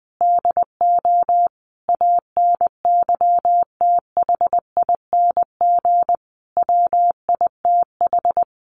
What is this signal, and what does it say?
Morse code, heard as a keyed tone, about 20 wpm, 700 Hz. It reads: DO ANYTHING WITH
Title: Sentences from Top 1000 Words 20wpm